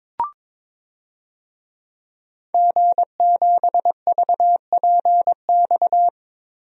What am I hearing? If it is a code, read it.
Morse code, heard as a keyed tone, about 22 wpm, 700 Hz. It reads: G7VPX
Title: DXCC-100 Call Signs - 5 Characters - Repeated 3x - 22wpm